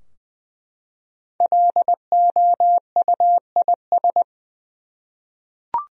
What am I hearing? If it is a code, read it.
Morse code, heard as a keyed tone, about 20 wpm, 700 Hz. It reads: LOUIS
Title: USA Common Names - Max 5 Letters - 20wpm